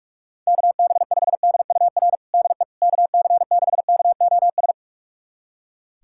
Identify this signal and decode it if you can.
Morse code, heard as a keyed tone, about 46 wpm, 690 Hz. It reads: KB5DUR DE KC6KOS
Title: USA Callsign DE Callsign - 6 Characters - 45wpm